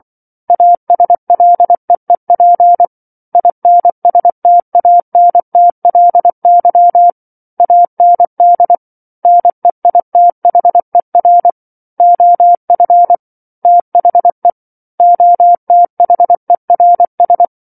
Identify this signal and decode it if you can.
Morse code, heard as a keyed tone, about 24 wpm, 700 Hz. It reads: ASLEEP INSTANTLY AND NEITHER OF THE OTHERS
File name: 07-Chapter-07 24 WPM